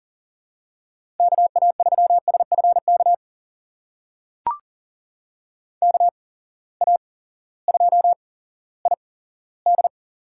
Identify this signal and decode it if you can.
Morse code, heard as a keyed tone, about 40 wpm, spaced as for 18 wpm, 700 Hz. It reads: KA3SFK KA2ID
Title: USA Call Signs - 6 Characters - 8x Character Spacing - 40wpm